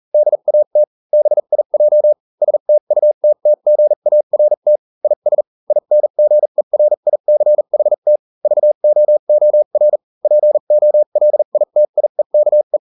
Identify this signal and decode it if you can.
Morse code, heard as a keyed tone, about 40 wpm, spaced as for 35 wpm, 600 Hz. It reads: DAT BIJ STUTTGART IS INGERICHT VOOR POLITIEKE